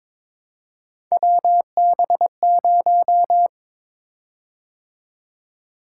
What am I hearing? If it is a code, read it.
Morse code, heard as a keyed tone, about 22 wpm, 700 Hz. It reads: WB0
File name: practice / 22wpm / Three Letter-Number Combinations - 22wpm